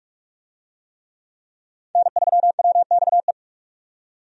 Morse code, heard as a keyed tone, about 43 wpm, 700 Hz. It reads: N3WXE
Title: USA Call Signs - 5 Characters - 45wpm